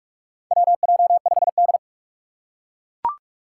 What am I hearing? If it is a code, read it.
Morse code, heard as a keyed tone, about 44 wpm, 690 Hz. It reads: WJ5D
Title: USA Call Signs - 4 Characters - 2x Character Spacing - 45wpm